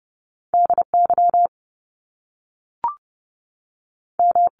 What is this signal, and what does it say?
Morse code, heard as a keyed tone, about 30 wpm, 700 Hz. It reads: DY M
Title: Top Two Letter Combinations in Words 30wpm